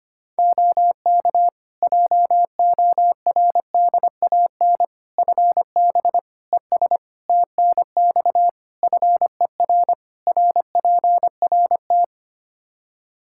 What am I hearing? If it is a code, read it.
Morse code, heard as a keyed tone, about 25 wpm, 700 Hz. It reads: OK JORDAN FB ES TNX FER RPRT
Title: Practice QSO - Exchange 2 - Info - 25wpm